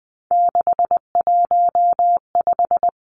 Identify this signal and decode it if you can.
Morse code, heard as a keyed tone, about 20 wpm, 700 Hz. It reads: 615
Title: Numbers - 3-Digit 20wpm